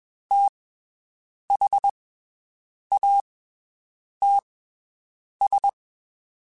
Morse code, heard as a keyed tone, about 21 wpm, spaced as for 7 wpm, 800 Hz. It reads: THATS
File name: (spaced) / QSOs Spaced-mp3s / WD0FBT 1 @20 sp